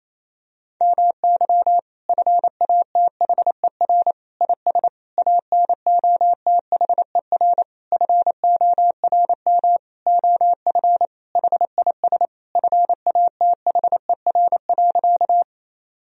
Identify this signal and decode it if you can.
Morse code, heard as a keyed tone, about 28 wpm, 700 Hz. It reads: MY FATHER IS ANOTHER FORM OF HIS FATHER.
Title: Sentences from Top 1000 Words - Review Part 1 - 28wpm